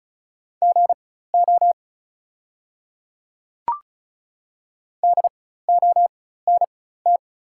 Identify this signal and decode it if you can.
Morse code, heard as a keyed tone, about 36 wpm, spaced as for 16 wpm, 690 Hz. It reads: GO DONT
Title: 4x Character Spacing - Top 500 Words - 35wpm